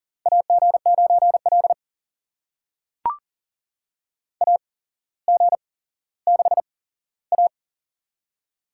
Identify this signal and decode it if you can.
Morse code, heard as a keyed tone, about 39 wpm, spaced as for 17 wpm, 690 Hz. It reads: AG9L AG6A
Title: USA Call Signs - 4 Characters - 8x Character Spacing - 40wpm